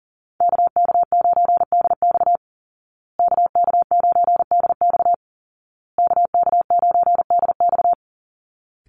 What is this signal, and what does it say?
Morse code, heard as a keyed tone, about 40 wpm, 700 Hz. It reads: KK9DX KK9DX KK9DX
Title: Call Signs Repeated 2x - 40wpm